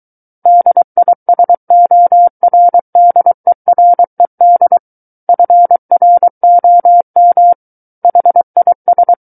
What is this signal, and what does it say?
Morse code, heard as a keyed tone, about 23 wpm, 700 Hz. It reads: DISORDERED FROM HIS
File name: Book One / 14-Chapter 14 - 23 WPM